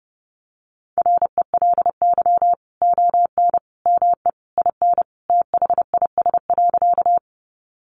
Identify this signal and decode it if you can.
Morse code, heard as a keyed tone, about 30 wpm, 700 Hz. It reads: RELY ON ME IN THIS.
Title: Sentences from Top 1200 Words 30wpm